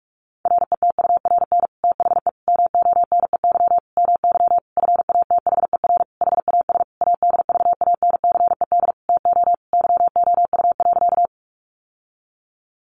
Morse code, heard as a keyed tone, about 44 wpm, 700 Hz. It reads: RETURN THE MONEY MY FATHER HAS ADVANCED TO YOU.
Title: Sentences from Top 1200 Words 45wpm